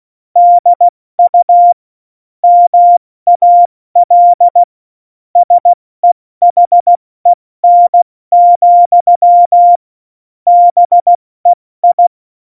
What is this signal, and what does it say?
Morse code, heard as a keyed tone, about 16 wpm, 700 Hz. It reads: DU MAL SEHEN, BEI
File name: Morsekurs / Witze / Witz 27 Juni 17, Tempo 70 F80